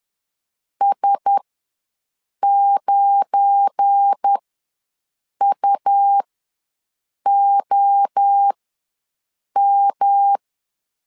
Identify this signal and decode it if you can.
Morse code, heard as a keyed tone, about 10 wpm, spaced as for 6 wpm, 800 Hz. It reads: S9UOM
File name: LEZIONI / 7 GF9Z / 7 GF9 ws 10wpm 4034